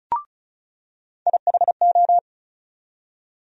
Morse code, heard as a keyed tone, about 34 wpm, 700 Hz. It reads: IHO
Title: USA Call Sign Postfixes - 3 Letters 35wpm